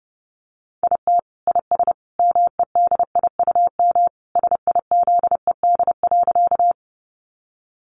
Morse code, heard as a keyed tone, about 30 wpm, 700 Hz. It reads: IT IS MEDIUM SIZED.
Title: Easy Phrases 30wpm